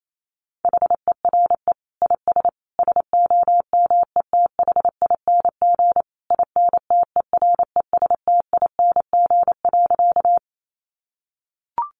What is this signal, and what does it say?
Morse code, heard as a keyed tone, about 28 wpm, 700 Hz. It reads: HERE IS SOMETHING INTERESTING.
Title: Sentences from Top 1000 Words 28wpm